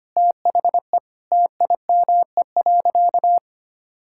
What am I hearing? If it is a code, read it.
Morse code, heard as a keyed tone, about 25 wpm, 700 Hz. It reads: THE TIME.
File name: practice / 25wpm / Sentences with 4-Word Phrases - 25wpm